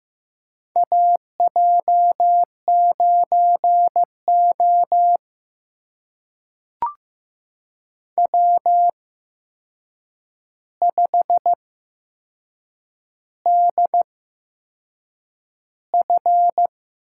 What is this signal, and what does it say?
Morse code, heard as a keyed tone, about 15 wpm, spaced as for 6 wpm, 700 Hz. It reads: AJ9O W5DF
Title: USA Call Signs - 8x Character Spacing - 15wpm